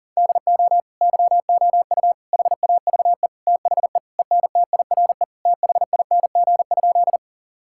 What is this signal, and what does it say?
Morse code, heard as a keyed tone, about 40 wpm, 700 Hz. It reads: DO YOU HAVE THE ENTIRE THING?
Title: Sentences from Top 1000 Words - Review Part 5 - 40wpm